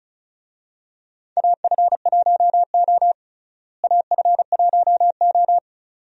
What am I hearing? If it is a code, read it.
Morse code, heard as a keyed tone, about 34 wpm, 700 Hz. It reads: AF1O AF1O
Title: USA Call Signs - 4 Characters - Repeated 2x - 35wpm